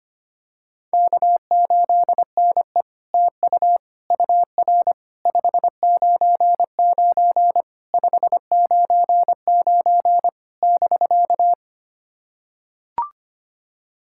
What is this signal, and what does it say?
Morse code, heard as a keyed tone, about 25 wpm, 700 Hz. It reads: K8NE TU UR 599 599 <BK>
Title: SOTA QSO Practice 25wpm